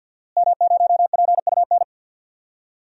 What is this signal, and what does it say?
Morse code, heard as a keyed tone, about 51 wpm, 700 Hz. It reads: M0PUN
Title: DXCC-100 Call Signs - 5 Characters - Repeated 2x - 50wpm